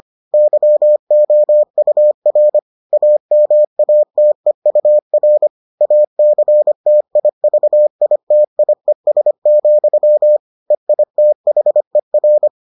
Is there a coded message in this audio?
Yes: Morse code from 0.3 to 12.5 s, heard as a keyed tone, about 25 wpm, 600 Hz. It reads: YOUR AMATEUR ACTIVITIES, EITHER